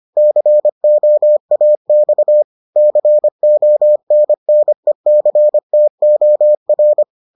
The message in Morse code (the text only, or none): COAX CONNECTOR